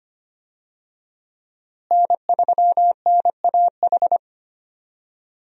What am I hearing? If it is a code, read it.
Morse code, heard as a keyed tone, about 25 wpm, 700 Hz. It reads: N3NAH
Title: USA Call Signs - 5 Characters - Repeated 2x - 25wpm